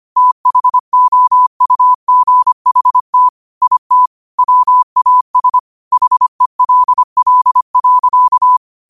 Morse code, heard as a keyed tone, about 25 wpm, 1000 Hz. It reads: THOUGHT IT WAS HELL.
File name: qotdfiles / 25wpm